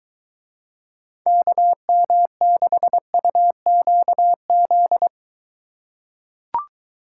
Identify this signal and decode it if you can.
Morse code, heard as a keyed tone, about 23 wpm, 700 Hz. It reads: KM6UQZ